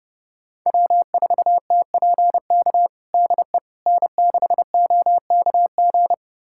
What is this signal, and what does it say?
Morse code, heard as a keyed tone, about 30 wpm, 700 Hz. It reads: W4TPK DE N6OKG
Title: USA Callsign DE Callsign - 5 Characters - 30wpm